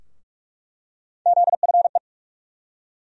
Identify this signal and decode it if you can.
Morse code, heard as a keyed tone, about 42 wpm, 700 Hz. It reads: ZFE
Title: USA Call Sign Postfixes - 3 Letters 45wpm